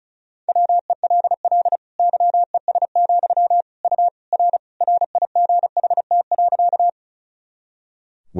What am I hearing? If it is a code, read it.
Morse code, heard as a keyed tone, about 36 wpm, 700 Hz. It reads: WELL YES, U R RIGHT.
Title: Sentences from Top 500 Words and Abbreviations 35wpm